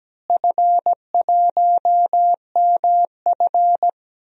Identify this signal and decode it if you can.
Morse code, heard as a keyed tone, about 17 wpm, 700 Hz. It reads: F1MF
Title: DXCC-100 Call Signs - 4 Characters - Repeated 3x - 17wpm